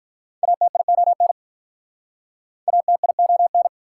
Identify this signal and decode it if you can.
Morse code, heard as a keyed tone, about 46 wpm, 700 Hz. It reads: ATION ATION